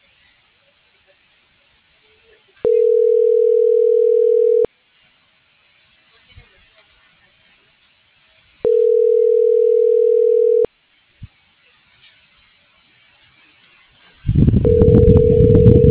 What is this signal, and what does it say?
Ringback tone (North America), 440 + 480 Hz.